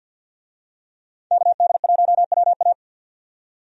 Morse code, heard as a keyed tone, about 52 wpm, 700 Hz. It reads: KD1WA
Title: USA Call Signs - 5 Characters - 50wpm